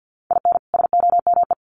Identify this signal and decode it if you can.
Morse code, heard as a keyed tone, about 50 wpm, 700 Hz. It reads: IN SOME
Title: Sets of 2 Words 50wpm